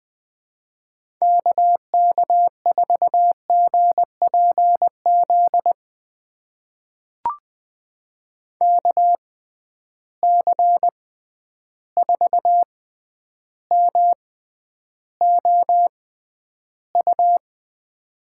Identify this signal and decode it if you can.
Morse code, heard as a keyed tone, about 21 wpm, spaced as for 10 wpm, 700 Hz. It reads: KK4GPZ KC4MOU
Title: USA Call Signs - 6 Characters - 6x Character Spacing - 20wpm